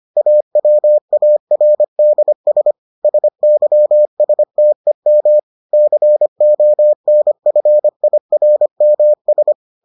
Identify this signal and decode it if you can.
Morse code, heard as a keyed tone, about 25 wpm, 600 Hz. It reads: AWARDS SYSTEM CONFIRMS